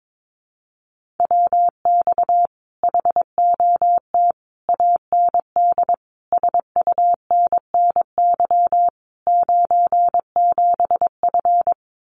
Morse code, heard as a keyed tone, about 22 wpm, 700 Hz. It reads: WX HOT AND SUNNY 97F
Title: Common QSO Phrases Encore 22wpm